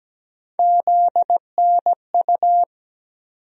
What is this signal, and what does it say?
Morse code, heard as a keyed tone, about 17 wpm, 700 Hz. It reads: ZNU
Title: USA Call Sign Postfixes - 3 Letters 17wpm